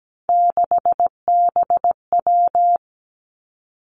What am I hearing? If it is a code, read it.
Morse code, heard as a keyed tone, about 17 wpm, 700 Hz. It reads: 6BW
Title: Three Challenging Letter-Number Combinations 17wpm